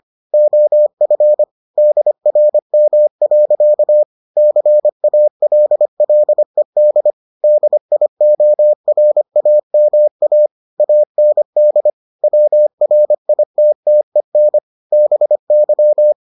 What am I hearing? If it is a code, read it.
Morse code, heard as a keyed tone, about 25 wpm, 600 Hz. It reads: OF DRM. CALLED DIORAMA AND WRITTEN BY